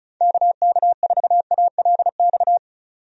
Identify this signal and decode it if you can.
Morse code, heard as a keyed tone, about 34 wpm, 700 Hz. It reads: KK4ALX
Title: USA Call Signs - 6 Characters - 35wpm